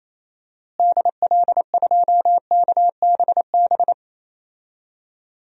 Morse code, heard as a keyed tone, about 28 wpm, 700 Hz. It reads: DL2KBB
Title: DXCC-100 Call Signs - 6 Characters - Repeated 3x - 28wpm